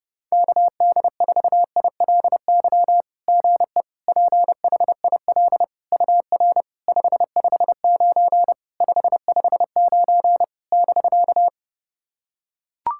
Morse code, heard as a keyed tone, about 30 wpm, 690 Hz. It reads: KD4ILY GE PHIL UR 559 559 <BK>
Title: SOTA QSO Practice 30wpm